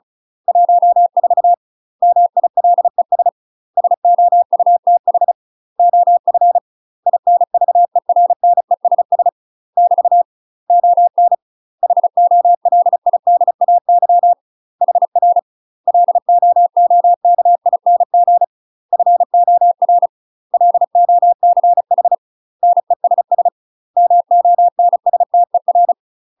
Morse code, heard as a keyed tone, about 36 wpm, 700 Hz. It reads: 14 MILES SOUTH OF INVERNESS <BT> ON HOLIDAY HR LOOKING FOR LOCH NESS MONSTER
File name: UK MASTER QSOs TRAINING / 35WPM / UK MASTER QSOs 0044